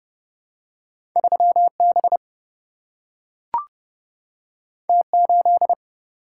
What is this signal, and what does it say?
Morse code, heard as a keyed tone, about 30 wpm, 700 Hz. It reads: 3B T8
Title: Single-Letter Single-Number 30wpm